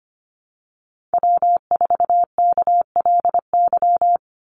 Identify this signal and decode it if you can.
Morse code, heard as a keyed tone, about 25 wpm, 700 Hz. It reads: W4KLY
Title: Call Signs Repeated 2x - 25wpm